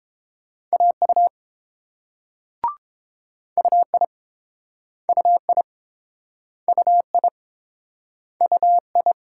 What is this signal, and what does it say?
Morse code, heard as a keyed tone, about 30 wpm, 700 Hz. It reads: AU UI UI UI UI